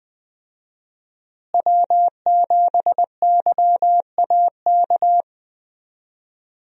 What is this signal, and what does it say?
Morse code, heard as a keyed tone, about 21 wpm, 700 Hz. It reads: W7YAK
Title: USA Call Signs - 5 Characters - Repeated 3x - 20wpm